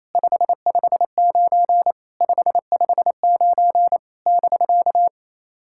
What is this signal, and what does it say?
Morse code, heard as a keyed tone, about 27 wpm, 690 Hz. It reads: 559 559 <BK>
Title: SOTA QSO Practice 28wpm